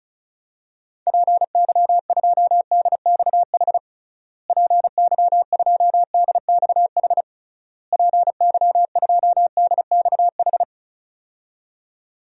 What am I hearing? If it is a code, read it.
Morse code, heard as a keyed tone, about 34 wpm, 700 Hz. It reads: PY2DXH PY2DXH PY2DXH
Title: DXCC-100 Call Signs - 6 Characters - Repeated 3x - 35wpm